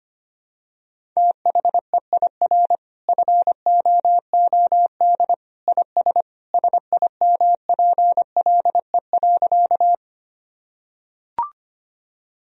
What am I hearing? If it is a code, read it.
Morse code, heard as a keyed tone, about 25 wpm, 700 Hz. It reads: THEIR FOOD IS SIMPLE.
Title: Sentences from Top 1000 Words - Review Part 1 - 25wpm